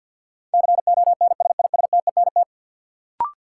Morse code, heard as a keyed tone, about 52 wpm, 700 Hz. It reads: CONSISTENT
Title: Sentences from Top 1500 Words - New Word Drill Alt - 50wpm